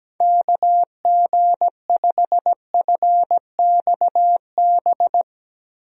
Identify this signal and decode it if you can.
Morse code, heard as a keyed tone, about 17 wpm, 700 Hz. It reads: KG5FXB
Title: USA Call Signs - 6 Characters - Repeated 3x - 17wpm